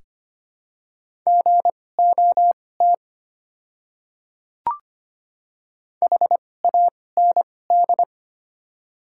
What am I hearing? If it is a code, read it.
Morse code, heard as a keyed tone, about 25 wpm, spaced as for 18 wpm, 700 Hz. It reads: GOT HAND
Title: 2x Character Spacing - Top 300 Words - 25wpm